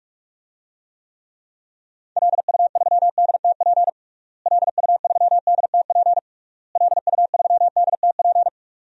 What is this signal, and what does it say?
Morse code, heard as a keyed tone, about 46 wpm, 700 Hz. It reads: LU3DTP LU3DTP LU3DTP
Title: DXCC-100 Call Signs - 6 Characters - Repeated 3x - 45wpm